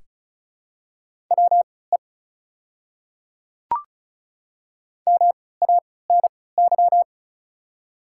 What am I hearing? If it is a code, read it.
Morse code, heard as a keyed tone, about 36 wpm, spaced as for 20 wpm, 690 Hz. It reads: WE MANY
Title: 3x Character Spacing - Top 100 Words - 35wpm